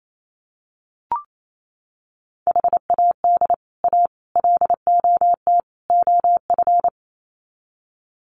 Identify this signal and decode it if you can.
Morse code, heard as a keyed tone, about 28 wpm, 700 Hz. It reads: HAD A LOT OF
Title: Sets of 4 Words 28wpm